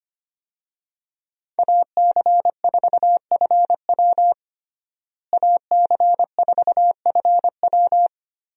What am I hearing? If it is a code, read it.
Morse code, heard as a keyed tone, about 25 wpm, 700 Hz. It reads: AC4FW AC4FW
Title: USA Call Signs - 5 Characters - Repeated 3x - 25wpm